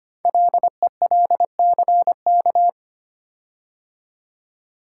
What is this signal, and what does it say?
Morse code, heard as a keyed tone, about 25 wpm, 700 Hz. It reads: LELCK
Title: Code Groups - 5 Letters - 25wpm